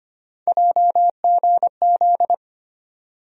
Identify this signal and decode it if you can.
Morse code, heard as a keyed tone, about 25 wpm, 700 Hz. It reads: JGZ
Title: USA Call Sign Postfixes - 3 Letters 25wpm